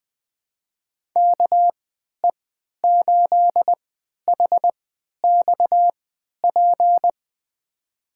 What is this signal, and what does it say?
Morse code, heard as a keyed tone, about 20 wpm, spaced as for 11 wpm, 700 Hz. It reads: KE8HXP
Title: USA Call Signs - 3x Character Spacing - 20wpm